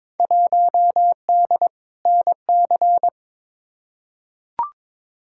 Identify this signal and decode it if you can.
Morse code, heard as a keyed tone, about 22 wpm, 700 Hz. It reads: 1D NC
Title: Field Day Exchanges Encore - 22wpm